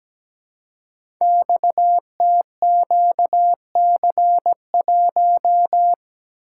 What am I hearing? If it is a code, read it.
Morse code, heard as a keyed tone, about 17 wpm, 700 Hz. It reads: XTQC1
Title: Code Groups - 5 Letter:Numbers - 17wpm